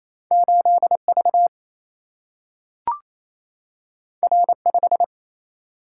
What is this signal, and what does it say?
Morse code, heard as a keyed tone, about 28 wpm, 700 Hz. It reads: 8V R5
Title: Single-Letter Single-Number 28wpm